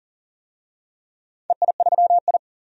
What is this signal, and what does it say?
Morse code, heard as a keyed tone, about 37 wpm, 700 Hz. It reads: EI3I
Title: DXCC-100 Call Signs - 4 Characters - 40wpm